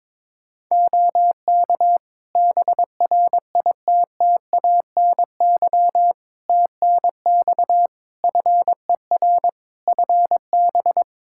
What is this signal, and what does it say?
Morse code, heard as a keyed tone, about 22 wpm, 700 Hz. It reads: OK BRITTANY TNX FER FB
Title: K1USN SST QSO Practice 22wpm